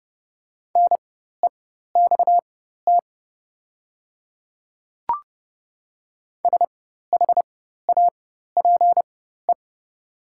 Morse code, heard as a keyed tone, about 30 wpm, spaced as for 14 wpm, 700 Hz. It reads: NEXT SHAPE
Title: 4x Character Spacing - Top 500 Words - 30wpm